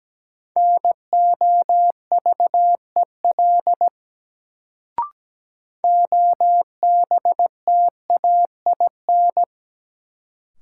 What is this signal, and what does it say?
Morse code, heard as a keyed tone, about 17 wpm, 700 Hz. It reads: NOVEL OBTAIN
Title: Sentences from Top 1300 Words - New Word Drill - 17wpm